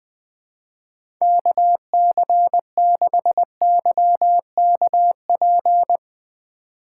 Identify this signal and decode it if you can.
Morse code, heard as a keyed tone, about 21 wpm, 700 Hz. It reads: KC6YKP
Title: USA Call Signs - 4x Character Spacing - 20wpm